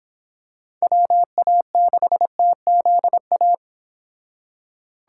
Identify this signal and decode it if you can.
Morse code, heard as a keyed tone, about 25 wpm, 700 Hz. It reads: WA6TZA